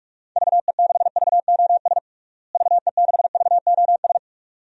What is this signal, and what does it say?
Morse code, heard as a keyed tone, about 41 wpm, 700 Hz. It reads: VE6VOS VE6VOS
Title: DXCC-100 Call Signs - 6 Characters - Repeated 2x - 45wpm